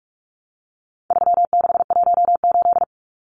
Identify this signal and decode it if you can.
Morse code, heard as a keyed tone, about 44 wpm, 700 Hz. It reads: 3618
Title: Numbers - 4-Digit 45wpm